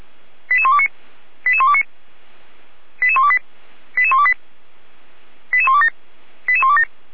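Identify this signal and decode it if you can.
Five-tone selective call (ZVEI), heard as quick runs of five tones, tones of about 70 ms. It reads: 88129 88129 88128 88128 88127 88127
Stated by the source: Five-tone ZVEI selective call received by SDR, from feuerwehr-orlamuende fire brigade server.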